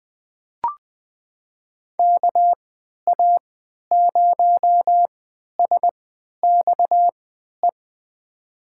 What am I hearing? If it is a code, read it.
Morse code, heard as a keyed tone, about 20 wpm, spaced as for 11 wpm, 700 Hz. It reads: KA0SXE
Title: USA Call Signs - 6 Characters - 3x Character Spacing - 20wpm